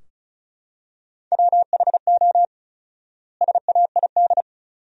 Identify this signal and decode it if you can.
Morse code, heard as a keyed tone, about 34 wpm, 700 Hz. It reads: WHO SAID
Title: 4x Word-Spacing - Sentences from Top 300 Words - 35wpm